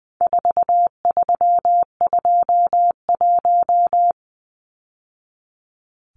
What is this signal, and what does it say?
Morse code, heard as a keyed tone, about 20 wpm, 700 Hz. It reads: 4321
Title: Numbers - 4-Digit 20wpm